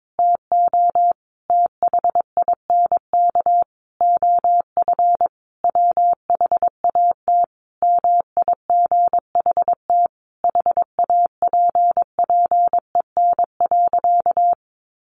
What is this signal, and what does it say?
Morse code, heard as a keyed tone, about 22 wpm, 700 Hz. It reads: TO THINK OF WHAT MIGHT HAPPEN.
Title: Sentences from Top 400 Words 22wpm